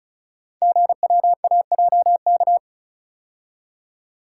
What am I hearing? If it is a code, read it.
Morse code, heard as a keyed tone, about 36 wpm, 700 Hz. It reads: GWAJK
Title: Code Groups - 5 Letter:Numbers - 35wpm